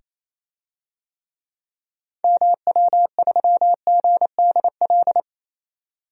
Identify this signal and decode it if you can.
Morse code, heard as a keyed tone, about 27 wpm, 700 Hz. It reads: MW3GDL
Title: DXCC-100 Call Signs - 6 Characters - 28wpm